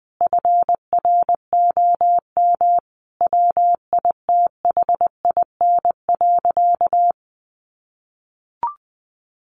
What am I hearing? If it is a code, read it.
Morse code, heard as a keyed tone, about 20 wpm, 700 Hz. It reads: FROM WITHIN.
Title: Sentences from Top 700 Words 20wpm